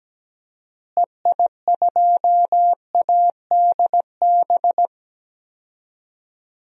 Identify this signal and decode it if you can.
Morse code, heard as a keyed tone, about 17 wpm, 700 Hz. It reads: EI2ADB
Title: DXCC-100 Call Signs - 6 Characters - Repeated 3x - 17wpm